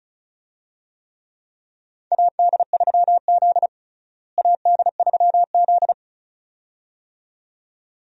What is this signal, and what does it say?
Morse code, heard as a keyed tone, about 36 wpm, 700 Hz. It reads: AD3Z AD3Z
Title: USA Call Signs - 4 Characters - Repeated 2x - 35wpm